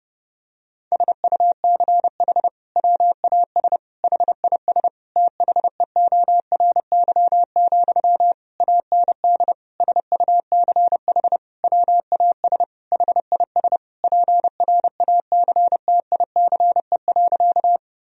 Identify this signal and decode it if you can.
Morse code, heard as a keyed tone, about 29 wpm, 700 Hz. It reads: SUCH WAS HIS THEORY, AND SUCH WAS HIS PRACTICE.
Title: Sentences from Top 1000 Words - Review Part 4 - 30wpm